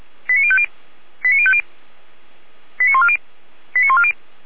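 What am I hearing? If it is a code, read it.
Five-tone selective call (ZVEI), heard as quick runs of five tones, tones of about 70 ms. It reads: 89050 89050 89140 89140
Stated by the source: ZVEI five-tone selective call 89187 received by SDR.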